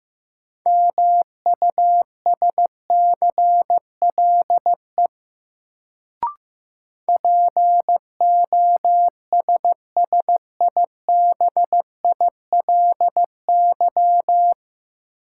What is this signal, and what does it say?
Morse code, heard as a keyed tone, about 15 wpm, 700 Hz. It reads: MUSCLE POSSIBILY
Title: Sentences from Top 1300 Words - New Word Drill - 15wpm